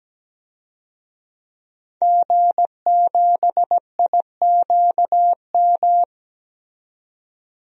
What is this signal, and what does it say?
Morse code, heard as a keyed tone, about 17 wpm, 700 Hz. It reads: G7IQM
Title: DXCC-100 Call Signs - 5 Characters - 17wpm